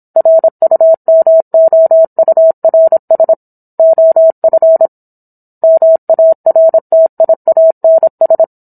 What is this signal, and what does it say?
Morse code, heard as a keyed tone, about 27 wpm, 650 Hz. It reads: RUMOURS OF MARTIANS